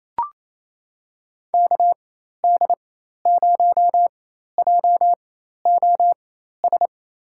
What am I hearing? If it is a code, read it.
Morse code, heard as a keyed tone, about 27 wpm, spaced as for 13 wpm, 700 Hz. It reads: KD0JOS
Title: USA Call Signs - 4x Character Spacing - 28wpm